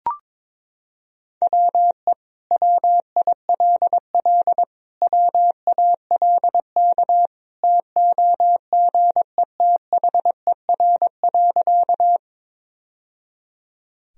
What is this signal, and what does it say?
Morse code, heard as a keyed tone, about 22 wpm, 700 Hz. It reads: WE WILL WALK TOGETHER.
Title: Sentences from Top 1000 Words - Review Part 1 - 22wpm